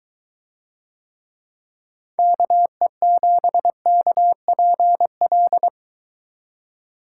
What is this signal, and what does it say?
Morse code, heard as a keyed tone, about 22 wpm, 700 Hz. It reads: KE7KPL